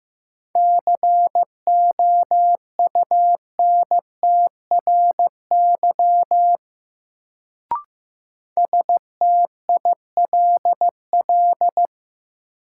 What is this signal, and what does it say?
Morse code, heard as a keyed tone, about 15 wpm, 700 Hz. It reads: COUNTRY STILL
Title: Sentences from Top 300 Words - New Word Drill - 15wpm